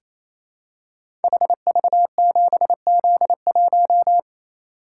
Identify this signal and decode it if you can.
Morse code, heard as a keyed tone, about 28 wpm, 700 Hz. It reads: HV7Z1
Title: Code Groups - 5 Letter:Numbers - 28wpm